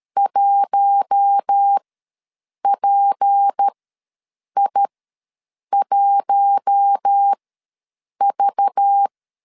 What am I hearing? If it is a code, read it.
Morse code, heard as a keyed tone, about 13 wpm, spaced as for 7 wpm, 800 Hz. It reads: 1PI1V
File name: LEZIONI / 6 PKX0 / 6 PK ws 12wpm 3116